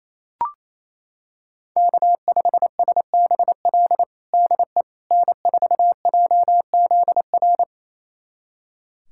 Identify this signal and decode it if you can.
Morse code, heard as a keyed tone, about 28 wpm, 700 Hz. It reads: K5SBL DE N4JZR
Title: USA Callsign DE Callsign - 5 Characters - 28wpm